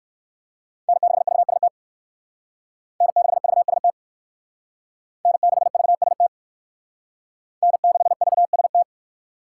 Morse code, heard as a keyed tone, about 65 wpm, 700 Hz. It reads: N6VST N6VST N6VST N6VST